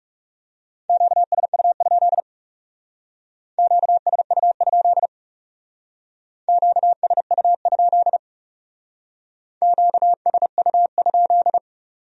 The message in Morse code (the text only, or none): QSU? QSU? QSU? QSU?